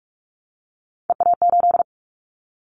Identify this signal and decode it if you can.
Morse code, heard as a keyed tone, about 46 wpm, 700 Hz. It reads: EA8
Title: Callsign Prefixes for Top 50 DXCC - 45wpm